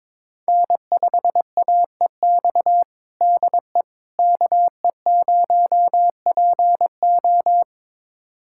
Morse code, heard as a keyed tone, about 21 wpm, 700 Hz. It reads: N5AEX DE KE0PO
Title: USA Callsign DE Callsign - 5 Characters - 22wpm